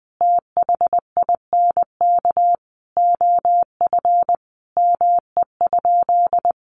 Morse code, heard as a keyed tone, about 20 wpm, 700 Hz. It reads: THINK OF ME?
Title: Sentences from Top 1100 Words 20wpm